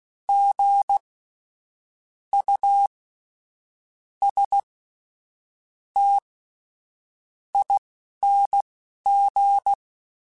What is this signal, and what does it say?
Morse code, heard as a keyed tone, about 16 wpm, spaced as for 7 wpm, 800 Hz. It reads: GUSTING
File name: Spaced Alphas / Spaced Alphas mp3s / Gs sp @10